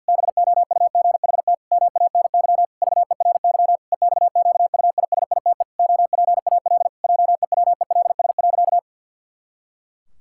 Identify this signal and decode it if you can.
Morse code, heard as a keyed tone, about 50 wpm, 700 Hz. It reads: BOUGHT MANY VERY EXQUISITE OPAL JEWELS.
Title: Pangram Challenge v2 - 50wpm